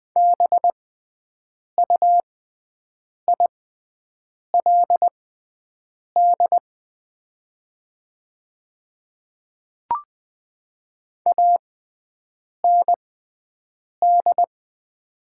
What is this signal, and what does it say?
Morse code, heard as a keyed tone, about 20 wpm, spaced as for 7 wpm, 690 Hz. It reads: BUILD AND
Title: 6x Character Spacing - Top 500 Words - 20wpm